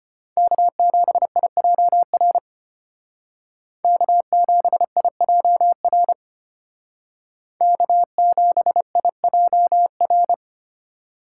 Morse code, heard as a keyed tone, about 31 wpm, 700 Hz. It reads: K7IJR K7IJR K7IJR
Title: Speed Racing - USA 1x3 Call Signs 25wpm